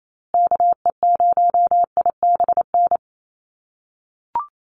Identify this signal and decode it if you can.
Morse code, heard as a keyed tone, about 28 wpm, 700 Hz. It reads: KE0IBN
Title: USA Call Signs Repeated 3x 28wpm